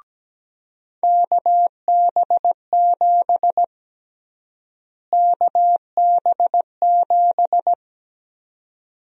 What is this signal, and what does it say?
Morse code, heard as a keyed tone, about 17 wpm, 700 Hz. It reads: KB7 KB7
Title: USA Call Sign Prefixes - Repeated 2x - 17wpm